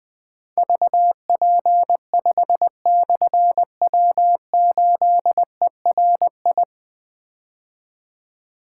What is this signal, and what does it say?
Morse code, heard as a keyed tone, about 20 wpm, 700 Hz. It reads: VP5/W8ERI
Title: Difficult Call Signs 20wpm